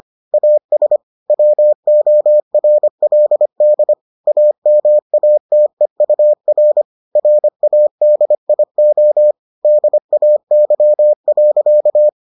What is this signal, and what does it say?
Morse code, heard as a keyed tone, about 25 wpm, 600 Hz. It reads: AS WORLD AMATEUR RADIO DAY.